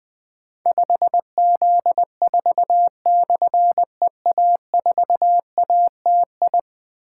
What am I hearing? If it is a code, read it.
Morse code, heard as a keyed tone, about 21 wpm, 700 Hz. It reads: 5Z4/EA4ATI
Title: Difficult Call Signs 20wpm